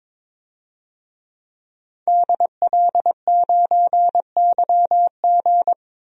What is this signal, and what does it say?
Morse code, heard as a keyed tone, about 22 wpm, 700 Hz. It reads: DL9YG
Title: DXCC-100 Call Signs - 5 Characters - Repeated 2x - 22wpm